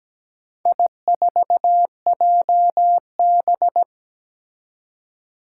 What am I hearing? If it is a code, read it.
Morse code, heard as a keyed tone, about 17 wpm, 700 Hz. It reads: I4JB
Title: Four Challenging Letter-Number Combinations 17wpm